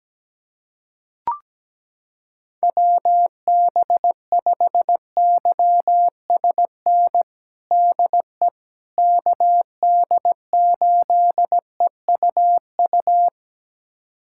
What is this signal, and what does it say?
Morse code, heard as a keyed tone, about 17 wpm, 700 Hz. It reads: WB5YSN DE KD8EUU
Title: USA Callsign DE Callsign - 6 Characters - 17wpm